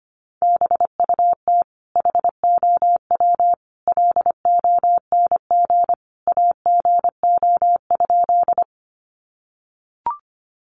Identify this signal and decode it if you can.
Morse code, heard as a keyed tone, about 25 wpm, 700 Hz. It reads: BUT HOW LONG AGO?
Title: Sentences from Top 500 Words 25wpm